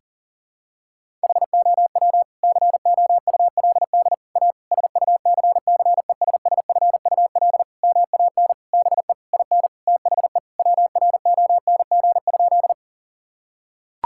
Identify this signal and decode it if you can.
Morse code, heard as a keyed tone, about 40 wpm, 700 Hz. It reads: HOW COULD A SUCCESSFUL MAN BE IN THE WRONG?
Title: Sentences from Top 1000 Words - Review Part 6 - 40wpm